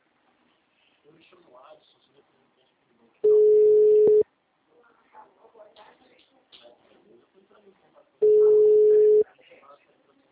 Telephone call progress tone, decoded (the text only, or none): ringback tone